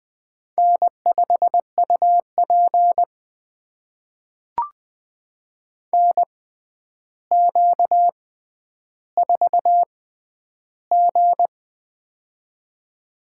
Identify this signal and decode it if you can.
Morse code, heard as a keyed tone, about 20 wpm, spaced as for 10 wpm, 700 Hz. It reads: N5UP NQ4G
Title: USA Call Signs - 4 Characters - 6x Character Spacing - 20wpm